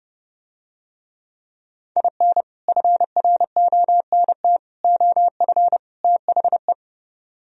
Morse code, heard as a keyed tone, about 30 wpm, 700 Hz. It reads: IN FRONT OF THE